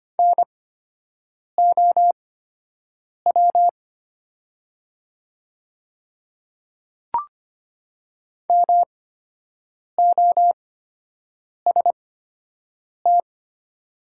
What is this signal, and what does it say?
Morse code, heard as a keyed tone, about 25 wpm, spaced as for 7 wpm, 700 Hz. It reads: NOW MOST
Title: 8x Character Spacing - Top 100 Words - 25wpm